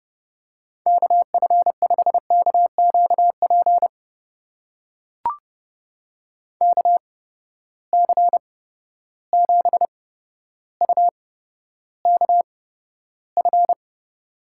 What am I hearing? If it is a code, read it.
Morse code, heard as a keyed tone, about 30 wpm, spaced as for 13 wpm, 700 Hz. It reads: KF5KQP KC7UKF
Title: USA Call Signs - 6 Characters - 8x Character Spacing - 30wpm